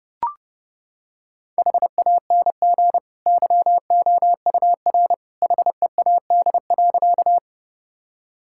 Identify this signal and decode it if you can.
Morse code, heard as a keyed tone, about 30 wpm, 700 Hz. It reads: HANG YOUR HEAD.
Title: Sentences from Top 1600 Words 30wpm